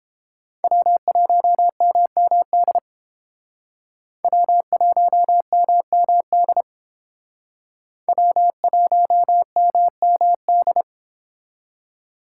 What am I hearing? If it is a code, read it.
Morse code, heard as a keyed tone, about 30 wpm, 700 Hz. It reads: W1MMD W1MMD W1MMD